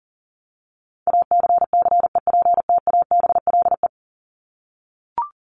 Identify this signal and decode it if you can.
Morse code, heard as a keyed tone, about 40 wpm, 700 Hz. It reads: ACCEPTABLE
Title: Top 1501-1750 Words 40wpm